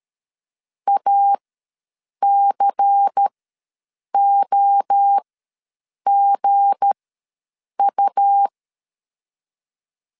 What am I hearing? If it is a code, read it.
Morse code, heard as a keyed tone, about 13 wpm, spaced as for 7 wpm, 800 Hz. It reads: ACOGU
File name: LEZIONI / 7 GF9Z / 7 GF ws 12wpm 3140